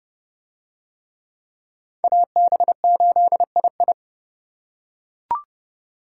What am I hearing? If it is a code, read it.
Morse code, heard as a keyed tone, about 30 wpm, 700 Hz. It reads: AB8II